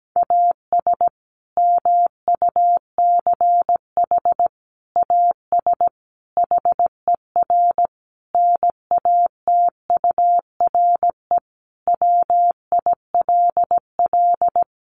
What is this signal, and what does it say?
Morse code, heard as a keyed tone, about 17 wpm, 700 Hz. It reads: AS MUCH AS HER NATURE WILL
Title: Sentences from Top 700 Words 17wpm